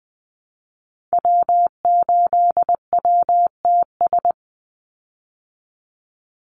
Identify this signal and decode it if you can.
Morse code, heard as a keyed tone, about 20 wpm, 700 Hz. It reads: W8WTS
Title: Call Signs Repeated 2x - 20wpm